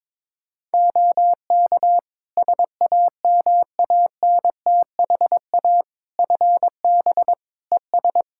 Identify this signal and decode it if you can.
Morse code, heard as a keyed tone, about 22 wpm, 700 Hz. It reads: OK SAMANTHA FB ES
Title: Practice QSO - Exchange 2 - Info - 22wpm